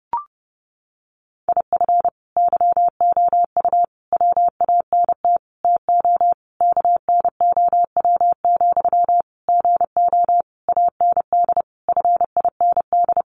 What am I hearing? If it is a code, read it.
Morse code, heard as a keyed tone, about 30 wpm, 700 Hz. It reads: IF YOU WANT TO KNOW, GO AND FIND
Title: Sentences from Top 200 Words 30wpm